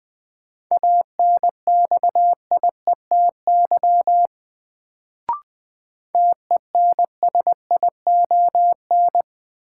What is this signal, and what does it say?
Morse code, heard as a keyed tone, about 21 wpm, 700 Hz. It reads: ANXIETY TENSION
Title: Sentences from Top 1600 Words - New Word Drill - 20wpm